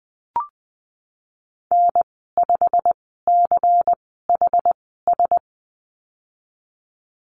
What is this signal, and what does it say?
Morse code, heard as a keyed tone, about 20 wpm, spaced as for 15 wpm, 700 Hz. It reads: N5CHS
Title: 147 Lesson 24g - Letter y - Call Signs 20wpm